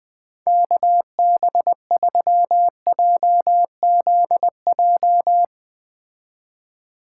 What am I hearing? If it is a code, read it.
Morse code, heard as a keyed tone, about 20 wpm, 700 Hz. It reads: KB3JZJ
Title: Speed Racing - USA 2x3 Call Signs 17wpm